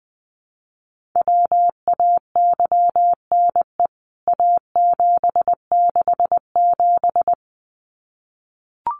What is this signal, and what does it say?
Morse code, heard as a keyed tone, about 20 wpm, 700 Hz. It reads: WAYNE A767
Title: CWT Exchange 20wpm